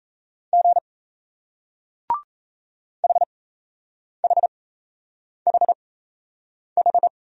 Morse code, heard as a keyed tone, about 37 wpm, spaced as for 8 wpm, 700 Hz. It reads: G HHHH